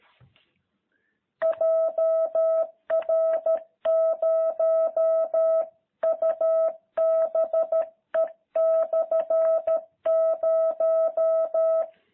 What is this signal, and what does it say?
Morse code, heard as a keyed tone, about 13 wpm, 660 Hz. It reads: JR0UBE/0
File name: jh8cbh / 2021cw